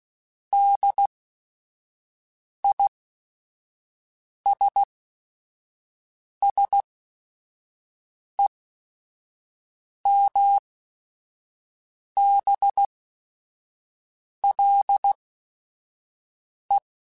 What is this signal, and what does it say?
Morse code, heard as a keyed tone, about 16 wpm, spaced as for 5 wpm, 800 Hz. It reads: DISSEMBLE